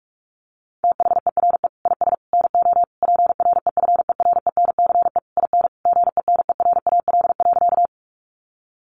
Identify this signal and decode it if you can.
Morse code, heard as a keyed tone, about 44 wpm, 700 Hz. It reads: THERE IS NO PREFERENCE IN GENERAL.
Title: Sentences from Top 1500 Words 45wpm